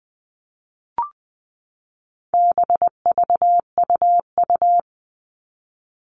Morse code, heard as a keyed tone, about 20 wpm, 700 Hz. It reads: BVUU
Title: All Cut Numbers - 4-Digit 20wpm